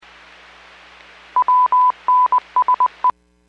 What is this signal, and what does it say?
Morse code, heard as a keyed tone, about 20 wpm, 1000 Hz. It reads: WNSE
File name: WPS / 02631 18-37 WPS 451.25 WPS